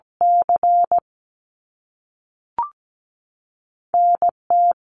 Morse code, heard as a keyed tone, about 17 wpm, spaced as for 5 wpm, 700 Hz. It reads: CNT
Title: Canadian Provinces and Territories 17wpm